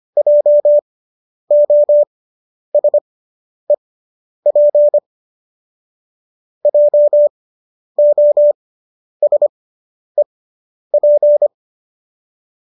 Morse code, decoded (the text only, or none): JOSEP JOSEP